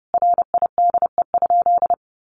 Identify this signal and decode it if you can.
Morse code, heard as a keyed tone, about 30 wpm, 700 Hz. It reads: RIDE?
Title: Sentences from Top 1200 Words 30wpm